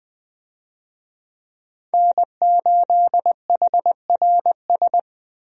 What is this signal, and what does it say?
Morse code, heard as a keyed tone, about 20 wpm, 700 Hz. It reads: N8HRS
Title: USA Call Signs - 5 Characters - Repeated 3x - 20wpm